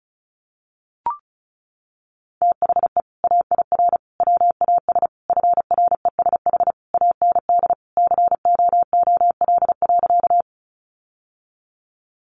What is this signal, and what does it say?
Morse code, heard as a keyed tone, about 34 wpm, 700 Hz. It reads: THE AIR WAS FRESH AND COOL.
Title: Sentences from Top 1200 Words 35wpm